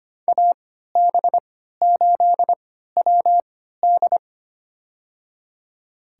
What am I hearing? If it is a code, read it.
Morse code, heard as a keyed tone, about 25 wpm, spaced as for 14 wpm, 700 Hz. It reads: AB8WD
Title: USA Call Signs - 5 Characters - 3x Character Spacing - 25wpm